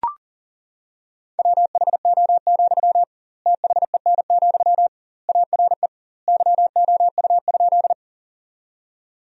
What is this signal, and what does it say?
Morse code, heard as a keyed tone, about 40 wpm, 700 Hz. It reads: WHO, THEN, ARE YOU?
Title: 2x Word-Spacing - Sentences from Top 100 Words - 40wpm